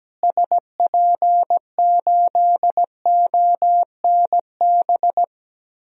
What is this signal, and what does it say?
Morse code, heard as a keyed tone, about 17 wpm, 700 Hz. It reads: SP8ONB
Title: DXCC-100 Call Signs - 6 Characters - Repeated 3x - 17wpm